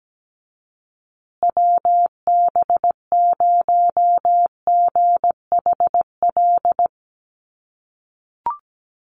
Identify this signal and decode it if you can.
Morse code, heard as a keyed tone, about 17 wpm, 700 Hz. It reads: WB0GHL
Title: USA Call Signs Repeated 2x 17wpm